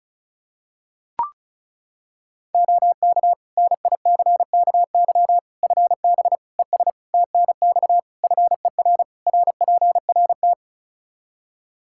Morse code, heard as a keyed tone, about 36 wpm, 700 Hz. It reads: OK NICKY FB ES TNX FER RPRT
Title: Practice QSO - Exchange 2 - Info - 35wpm